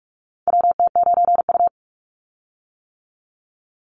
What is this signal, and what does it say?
Morse code, heard as a keyed tone, about 44 wpm, 700 Hz. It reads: WT9U
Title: Call Signs - 45wpm